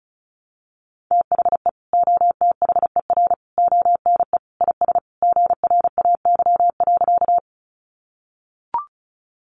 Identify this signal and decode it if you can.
Morse code, heard as a keyed tone, about 34 wpm, 700 Hz. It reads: THE OTHER ONE IS GRAY.
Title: Additional Easy Phrases 35wpm